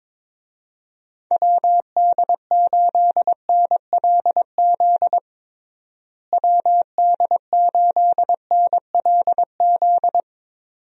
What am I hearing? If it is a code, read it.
Morse code, heard as a keyed tone, about 22 wpm, 700 Hz. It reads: WD8NLZ WD8NLZ
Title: USA Call Signs - 6 Characters - Repeated 2x - 22wpm